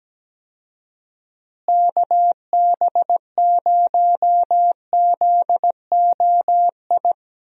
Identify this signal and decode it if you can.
Morse code, heard as a keyed tone, about 17 wpm, 700 Hz. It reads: KB0ZOI
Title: USA Call Signs - 6 Characters - Repeated 2x - 17wpm